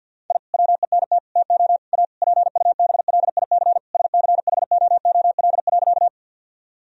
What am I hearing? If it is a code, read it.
Morse code, heard as a keyed tone, about 46 wpm, 700 Hz. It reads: I WENT TO A PUBLIC SCHOOL.
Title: Sentences from Top 1000 Words - Review Part 3 - 50wpm